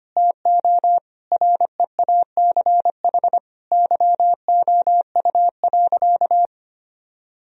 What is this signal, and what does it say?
Morse code, heard as a keyed tone, about 25 wpm, 700 Hz. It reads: TO REACH YOU.
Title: Sentences from Top 1000 Words - Review Part 2 - 25wpm